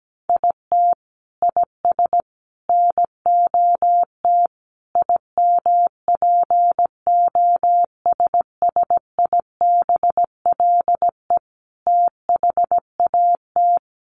Morse code, heard as a keyed tone, about 17 wpm, 700 Hz. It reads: IT IS NOT IMPOSSIBLE THAT
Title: Sentences from Top 1100 Words 17wpm